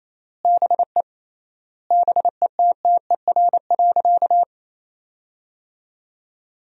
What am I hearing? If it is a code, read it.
Morse code, heard as a keyed tone, about 28 wpm, 700 Hz. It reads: BE BETTER.
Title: 3x Word-Spacing - Sentences from Top 400 Words - 28wpm